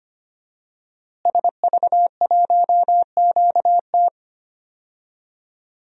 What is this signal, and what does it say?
Morse code, heard as a keyed tone, about 25 wpm, 700 Hz. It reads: SV1QT
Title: DXCC-100 Call Signs - 5 Characters - Repeated 3x - 25wpm